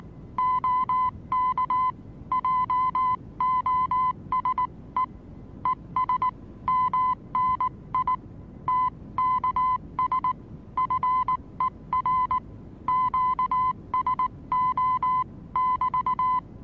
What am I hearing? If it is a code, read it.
Morse code, heard as a keyed tone, about 19 wpm, 1010 Hz. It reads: OK JOSE ES MNI TKS FER QSO<BT>
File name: repos / Morsekurs / DJ4UF Morsen26